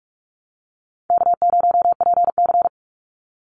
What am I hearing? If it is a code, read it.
Morse code, heard as a keyed tone, about 44 wpm, 700 Hz. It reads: K0PC
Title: Easy Call Signs - 45wpm